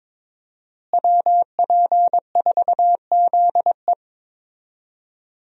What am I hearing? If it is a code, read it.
Morse code, heard as a keyed tone, about 21 wpm, 700 Hz. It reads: WP4ZE
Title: USA Call Signs - 5 Characters - Repeated 3x - 22wpm